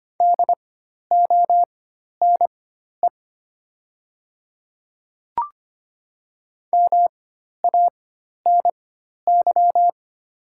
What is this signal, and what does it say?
Morse code, heard as a keyed tone, about 25 wpm, spaced as for 12 wpm, 700 Hz. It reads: DONE MANY
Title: 4x Character Spacing - Top 500 Words - 25wpm